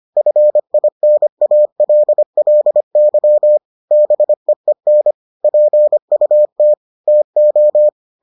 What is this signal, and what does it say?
Morse code, heard as a keyed tone, about 25 wpm, 600 Hz. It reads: FINALLY BEEN PUT TO